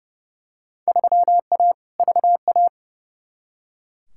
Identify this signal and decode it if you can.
Morse code, heard as a keyed tone, about 31 wpm, 700 Hz. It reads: 3A VA
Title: Field Day Exchanges Encore - 30wpm